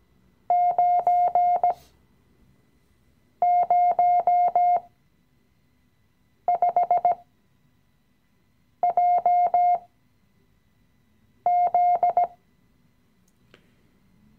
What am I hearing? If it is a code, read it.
Morse code, heard as a keyed tone, about 17 wpm, spaced as for 5 wpm, 710 Hz. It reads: 905JZ
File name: CW-Online-Kurs / Anfänger / Lektion 017